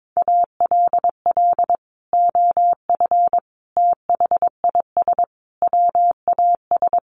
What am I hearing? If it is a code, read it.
Morse code, heard as a keyed tone, about 22 wpm, 700 Hz. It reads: ALL OF THIS WAS
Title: Sentences from Top 500 Words 22wpm